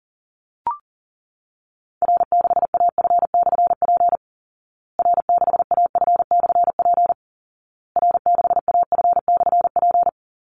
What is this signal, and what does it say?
Morse code, heard as a keyed tone, about 37 wpm, 700 Hz. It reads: R6AF/P R6AF/P R6AF/P
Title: Call Signs Repeated 2x - 40wpm